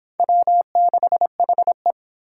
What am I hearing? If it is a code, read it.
Morse code, heard as a keyed tone, about 26 wpm, 700 Hz. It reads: W6HE